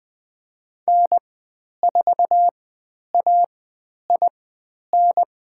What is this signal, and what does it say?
Morse code, heard as a keyed tone, about 20 wpm, spaced as for 10 wpm, 700 Hz. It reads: N4AIN
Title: 095 Lesson 16g - Letter c - Call Signs 20wpm